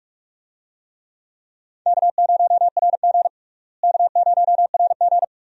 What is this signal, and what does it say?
Morse code, heard as a keyed tone, about 42 wpm, 700 Hz. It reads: K0RG K0RG
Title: USA Call Signs - 4 Characters - Repeated 2x - 45wpm